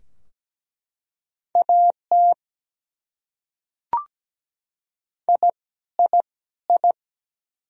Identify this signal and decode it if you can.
Morse code, heard as a keyed tone, about 17 wpm, 700 Hz. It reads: AT I I I
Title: Top 100 Words Repeated 17wpm